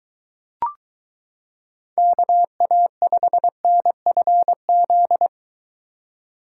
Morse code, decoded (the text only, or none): KA5NFZ